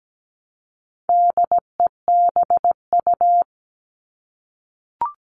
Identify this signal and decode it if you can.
Morse code, heard as a keyed tone, about 17 wpm, 700 Hz. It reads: DEBU
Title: All Cut Numbers - 4-Digit 17wpm